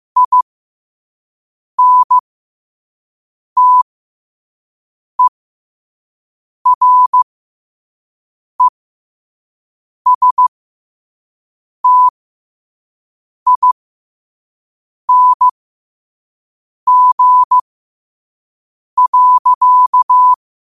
Morse code, heard as a keyed tone, about 15 wpm, spaced as for 5 wpm, 1000 Hz. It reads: INTERESTING.
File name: 5wpm